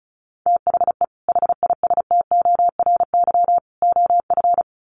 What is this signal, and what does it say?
Morse code, heard as a keyed tone, about 34 wpm, 700 Hz. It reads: THE HISTORY OF
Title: Sets of 3 Words 35wpm